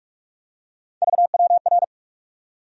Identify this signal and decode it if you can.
Morse code, heard as a keyed tone, about 42 wpm, 700 Hz. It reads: VWR
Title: USA Call Sign Postfixes - 3 Letters 45wpm